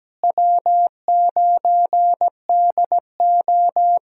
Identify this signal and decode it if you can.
Morse code, heard as a keyed tone, about 17 wpm, 700 Hz. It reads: W9DO
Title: USA Call Signs - 4 Characters - Repeated 3x - 17wpm